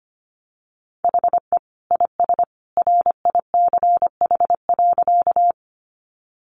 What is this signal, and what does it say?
Morse code, heard as a keyed tone, about 25 wpm, 700 Hz. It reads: HE IS RICH.
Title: Sentences from Top 1100 Words 25wpm